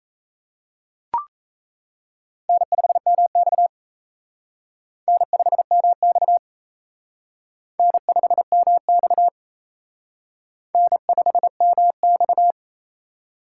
Morse code, heard as a keyed tone, about 32 wpm, 700 Hz. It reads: N5MX N5MX N5MX N5MX
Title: Speed Racing - USA 1x2 Call Signs 28wpm